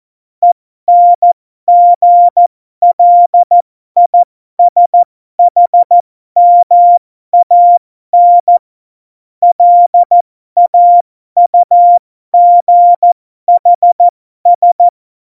Morse code, heard as a keyed tone, about 14 wpm, spaced as for 12 wpm, 700 Hz. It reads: ENGLISHMAN LAUGHS
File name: Morsekurs / Witze